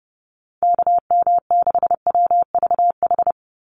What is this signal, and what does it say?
Morse code, heard as a keyed tone, about 30 wpm, 700 Hz. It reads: KM6WVH
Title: USA Call Signs 30wpm